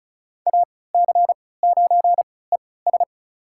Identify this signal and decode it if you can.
Morse code, heard as a keyed tone, about 36 wpm, spaced as for 20 wpm, 700 Hz. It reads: AC9ES
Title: USA Call Signs - 3x Character Spacing - 35wpm